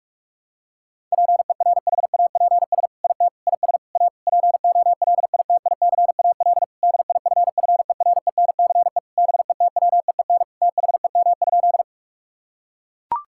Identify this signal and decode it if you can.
Morse code, heard as a keyed tone, about 41 wpm, 700 Hz. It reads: PERHAPS IT IS A POLITICAL DIFFERENCE BETWEEN THEM?
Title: Sentences from Top 1000 Words - Review Part 5 - 45wpm